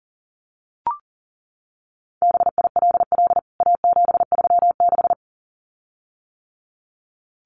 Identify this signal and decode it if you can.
Morse code, heard as a keyed tone, about 40 wpm, 700 Hz. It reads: BILL A736
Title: CWT Exchange 40wpm